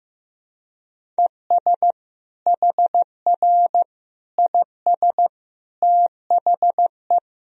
Morse code, heard as a keyed tone, about 15 wpm, 700 Hz. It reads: ES HR IS THE
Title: Sentences from Top 500 Words and Abbreviations 15wpm